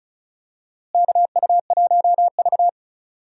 Morse code, heard as a keyed tone, about 34 wpm, 700 Hz. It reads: KU1V
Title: USA Call Signs - 4 Characters - Repeated 3x - 35wpm